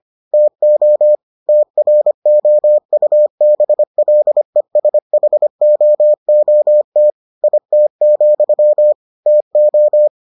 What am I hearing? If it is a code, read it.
Morse code, heard as a keyed tone, about 25 wpm, 600 Hz. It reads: TO TROUBLESHOOT IT, TO